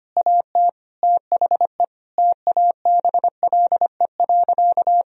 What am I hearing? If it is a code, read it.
Morse code, heard as a keyed tone, about 25 wpm, 700 Hz. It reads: AT THE TABLE.
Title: Sentences from Top 1000 Words - Review Part 2 - 25wpm